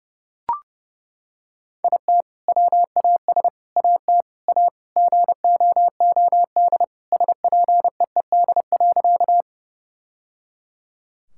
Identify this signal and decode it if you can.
Morse code, heard as a keyed tone, about 30 wpm, 700 Hz. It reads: IT WAS AT A GOOD SPEED.
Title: Sentences from Top 1000 Words - Review Part 5 - 30wpm